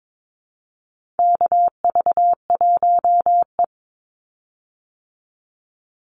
Morse code, heard as a keyed tone, about 22 wpm, 700 Hz. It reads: KV1E
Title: Call Signs Repeated 2x - 22wpm